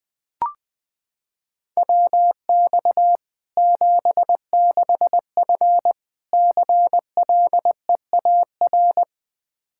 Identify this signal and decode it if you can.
Morse code, heard as a keyed tone, about 21 wpm, 700 Hz. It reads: WX 76F CLEAR
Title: K1USN SST QSO Practice 20wpm